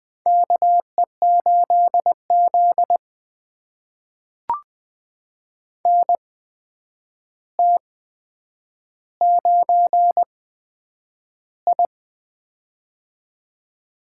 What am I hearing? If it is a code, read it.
Morse code, heard as a keyed tone, about 20 wpm, spaced as for 9 wpm, 700 Hz. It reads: KE8Z NT9I
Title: USA Call Signs - 4 Characters - 8x Character Spacing - 20wpm